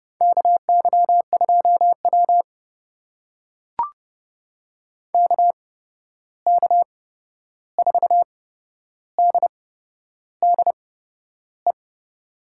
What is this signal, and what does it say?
Morse code, heard as a keyed tone, about 32 wpm, spaced as for 11 wpm, 700 Hz. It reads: KY2W KK4DDE
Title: USA Call Signs - 8x Character Spacing - 30wpm